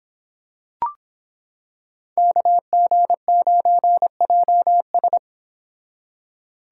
Morse code, heard as a keyed tone, about 26 wpm, 700 Hz. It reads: KG9JS